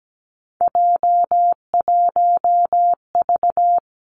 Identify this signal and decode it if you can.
Morse code, heard as a keyed tone, about 17 wpm, 700 Hz. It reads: J1V
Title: Three Challenging Letter-Number Combinations 17wpm